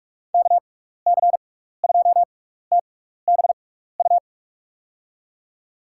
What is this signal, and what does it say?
Morse code, heard as a keyed tone, about 42 wpm, spaced as for 15 wpm, 700 Hz. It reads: KC2TBU
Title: USA Call Signs - 6 Characters - 6x Character Spacing - 45wpm